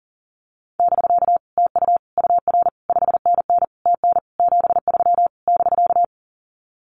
Morse code, heard as a keyed tone, about 40 wpm, 700 Hz. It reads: <BK> TU UR 5NN TN 73 <BK>
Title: Common QSO Phrases Encore 40wpm